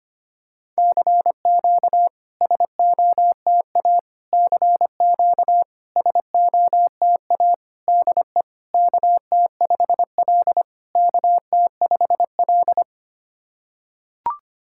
Morse code, heard as a keyed tone, about 25 wpm, 690 Hz. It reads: CQ SOTA CQ SOTA DE KT5L KT5L
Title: SOTA QSO Practice 25wpm